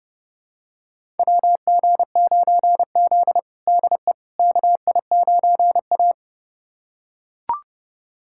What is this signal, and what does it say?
Morse code, heard as a keyed tone, about 30 wpm, 700 Hz. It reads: WG9Z DE KI9A
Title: USA Callsign DE Callsign - 4 Characters - 30wpm